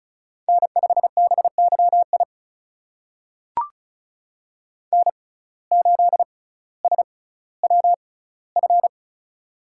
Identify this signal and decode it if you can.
Morse code, heard as a keyed tone, about 36 wpm, spaced as for 18 wpm, 700 Hz. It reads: N5BYI N8SWF
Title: USA Call Signs - 5 Characters - 6x Character Spacing - 35wpm